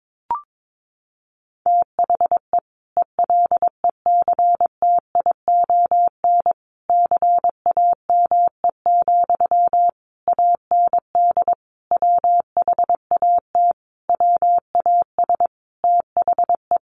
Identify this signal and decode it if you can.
Morse code, heard as a keyed tone, about 22 wpm, 700 Hz. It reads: THE ELECTION CAME, AND WHAT WAS THE
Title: Sentences from Top 1500 Words 22wpm